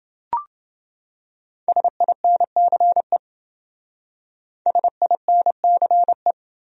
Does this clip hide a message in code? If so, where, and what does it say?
Yes: Morse code from 1.7 to 6.3 s, heard as a keyed tone, about 28 wpm, 700 Hz. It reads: SINCE SINCE